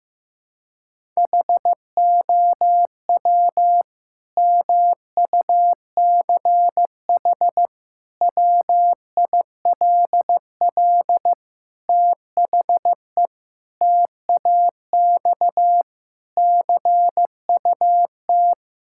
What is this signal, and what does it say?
Morse code, heard as a keyed tone, about 15 wpm, 700 Hz. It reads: HOW MUCH WILL THE TAX CUT
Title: Sentences from Top 1000 Words - Review Part 5 - 15wpm